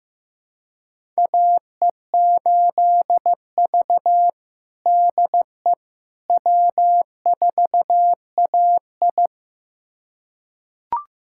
Morse code, heard as a keyed tone, about 15 wpm, 700 Hz. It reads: AE8V DE W4AI
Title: USA Callsign DE Callsign - 4 Characters - 15wpm